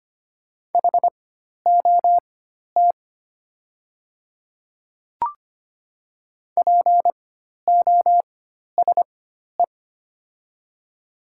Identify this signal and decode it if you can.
Morse code, heard as a keyed tone, about 25 wpm, spaced as for 12 wpm, 700 Hz. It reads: HOT POSE
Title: 4x Character Spacing - Top 500 Words - 25wpm